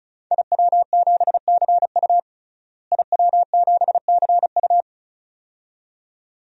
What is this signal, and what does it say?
Morse code, heard as a keyed tone, about 36 wpm, 700 Hz. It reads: IW7CU IW7CU
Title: DXCC-100 Call Signs - 5 Characters - Repeated 2x - 35wpm